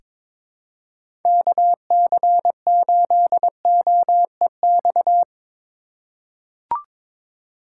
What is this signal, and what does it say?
Morse code, heard as a keyed tone, about 22 wpm, 700 Hz. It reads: KC8OEX
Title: USA Call Signs - 6 Characters - 4x Character Spacing - 22wpm